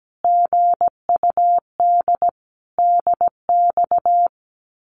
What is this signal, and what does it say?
Morse code, heard as a keyed tone, about 17 wpm, 700 Hz. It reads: GUD DX
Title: Common QSO Phrases Encore 17wpm